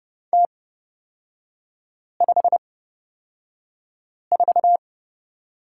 Morse code, heard as a keyed tone, about 30 wpm, spaced as for 5 wpm, 700 Hz. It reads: T54
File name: Lesson 37e - Letter x - Three Characters 30wpm